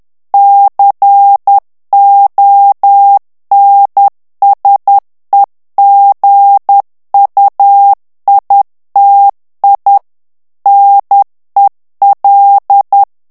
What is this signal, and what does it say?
Morse code, heard as a keyed tone, about 10 wpm, 800 Hz. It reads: CONSEGUITI NEL
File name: ITALIEN texte clair 2 vitesse 10 mots minute